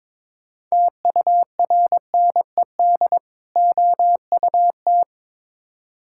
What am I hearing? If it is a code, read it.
Morse code, heard as a keyed tone, about 22 wpm, 700 Hz. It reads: TURNED OUT
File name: practice / 22wpm / Build A Phrase - 5-Word Phrases - 22wpm